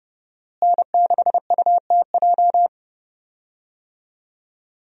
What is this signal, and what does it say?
Morse code, heard as a keyed tone, about 30 wpm, 700 Hz. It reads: N6UTJ
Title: USA Call Signs - 5 Characters - Repeated 2x - 30wpm